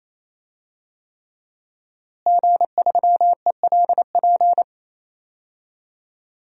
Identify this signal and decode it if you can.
Morse code, heard as a keyed tone, about 27 wpm, 700 Hz. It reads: G3ELP
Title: DXCC-100 Call Signs - 5 Characters - Repeated 3x - 28wpm